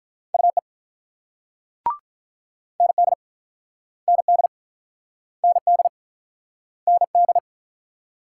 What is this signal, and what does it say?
Morse code, heard as a keyed tone, about 48 wpm, 690 Hz. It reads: UE ND ND ND ND